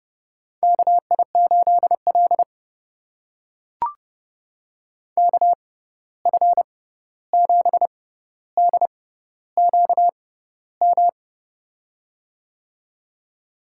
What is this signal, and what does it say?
Morse code, heard as a keyed tone, about 30 wpm, spaced as for 14 wpm, 700 Hz. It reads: KI8L KF7DQM
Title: USA Call Signs - 6x Character Spacing - 30wpm